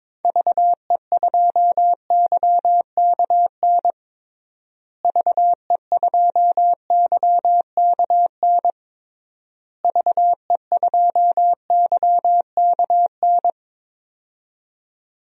Morse code, heard as a keyed tone, about 21 wpm, 700 Hz. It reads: VE2YKN VE2YKN VE2YKN
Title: DXCC-100 Call Signs - 6 Characters - Repeated 3x - 22wpm